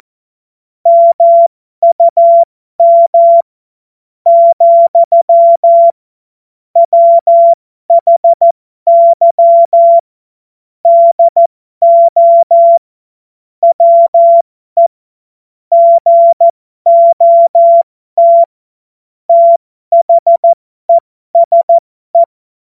Morse code, heard as a keyed tone, about 14 wpm, spaced as for 12 wpm, 680 Hz. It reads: MUM , WHY DO WE GOT THESE
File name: Morsekurs / Witze